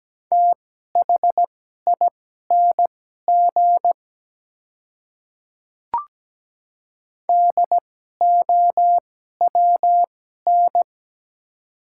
Morse code, heard as a keyed tone, about 17 wpm, spaced as for 12 wpm, 700 Hz. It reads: THING DOWN
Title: 2x Character Spacing - Top 100 Words - 17wpm